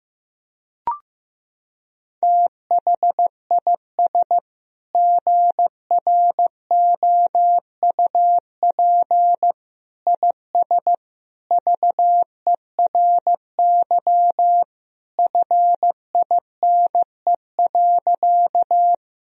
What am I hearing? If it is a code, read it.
Morse code, heard as a keyed tone, about 14 wpm, 700 Hz. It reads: THIS GROUP IS VERY FINE.
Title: Sentences from Top 1000 Words - Review Part 1 - 15wpm